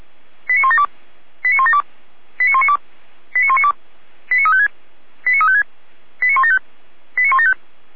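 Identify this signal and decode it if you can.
Five-tone selective call (ZVEI), heard as quick runs of five tones, tones of about 70 ms. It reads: 89172 89172 89192 89192 89356 89356 89175 89175
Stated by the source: Five-tone ZVEI selective call received on an SDR.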